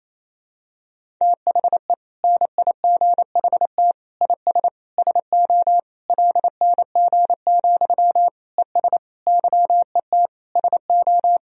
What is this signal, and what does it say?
Morse code, heard as a keyed tone, about 28 wpm, 700 Hz. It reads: THE NIGHT IS SO LNG, ES YET SO
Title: Sentences from Top 500 Words and Abbreviations 28wpm